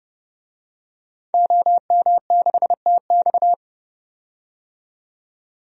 Morse code, heard as a keyed tone, about 30 wpm, 700 Hz. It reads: OM6TX
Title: DXCC-100 Call Signs - 5 Characters - 30wpm